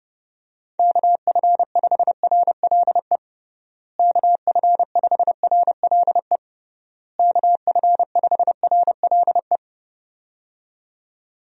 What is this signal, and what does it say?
Morse code, heard as a keyed tone, about 30 wpm, 700 Hz. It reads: KF5RLE KF5RLE KF5RLE
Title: USA Call Signs - 6 Characters - Repeated 3x - 30wpm